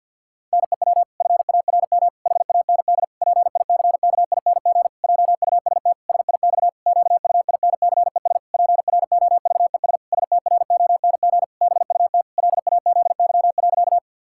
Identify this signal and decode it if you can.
Morse code, heard as a keyed tone, about 50 wpm, 700 Hz. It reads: NEW FARM HAND PICKING JUST SIX QUINCES PROVES STRONG BUT LAZY.
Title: Pangram Challenge v2 - 50wpm